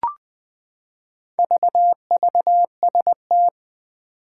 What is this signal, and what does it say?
Morse code, heard as a keyed tone, about 20 wpm, 700 Hz. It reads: VVST
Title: Four Challenging Letter-Number Combinations 20wpm